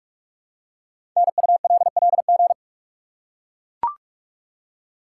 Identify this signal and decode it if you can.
Morse code, heard as a keyed tone, about 44 wpm, 700 Hz. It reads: NULLG
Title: Code Groups - 5 Letter:Numbers - 45wpm